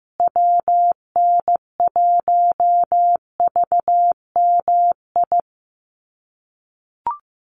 Morse code, heard as a keyed tone, about 15 wpm, 700 Hz. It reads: WN1VMI
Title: USA Call Signs Repeated 3x 15wpm